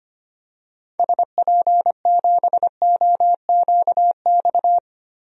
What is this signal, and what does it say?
Morse code, heard as a keyed tone, about 25 wpm, 700 Hz. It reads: SP7OQX
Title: DXCC-100 Call Signs - 6 Characters - Repeated 2x - 25wpm